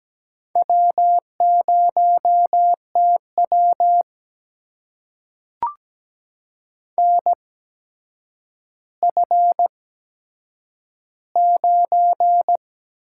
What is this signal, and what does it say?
Morse code, heard as a keyed tone, about 17 wpm, spaced as for 8 wpm, 700 Hz. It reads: W0TW NF9
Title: USA Call Signs - 4 Characters - 8x Character Spacing - 17wpm